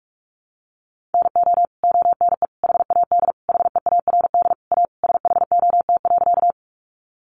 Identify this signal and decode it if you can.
Morse code, heard as a keyed tone, about 44 wpm, 700 Hz. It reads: NO ONE HAD HEARD A SHOT.
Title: Sentences from Top 1100 Words 45wpm